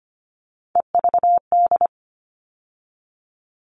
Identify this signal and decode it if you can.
Morse code, heard as a keyed tone, about 25 wpm, 700 Hz. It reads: EVD
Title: All Cut Numbers - 3-Digit 25wpm